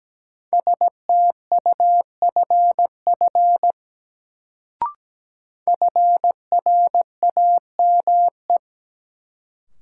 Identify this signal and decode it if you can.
Morse code, heard as a keyed tone, about 17 wpm, 700 Hz. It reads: STUFF FRAME
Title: Sentences from Top 1400 Words - New Word Drill Alt - 17wpm